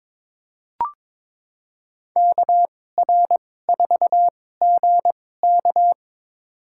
Morse code, heard as a keyed tone, about 22 wpm, spaced as for 16 wpm, 700 Hz. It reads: KR4GK
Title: USA Call Signs - 2x Character Spacing - 22wpm